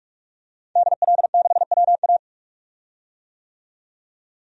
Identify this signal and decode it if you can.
Morse code, heard as a keyed tone, about 42 wpm, 700 Hz. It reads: DL6WA
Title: DXCC-100 Call Signs - 5 Characters - 45wpm